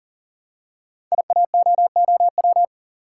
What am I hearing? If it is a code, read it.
Morse code, heard as a keyed tone, about 41 wpm, 700 Hz. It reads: IAOOW
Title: Code Groups - 5 Letters - 40wpm